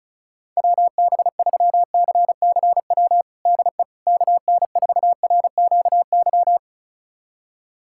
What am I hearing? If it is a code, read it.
Morse code, heard as a keyed tone, about 36 wpm, 700 Hz. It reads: WB3CCW DE KN4RQY
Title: USA Callsign DE Callsign - 6 Characters - 35wpm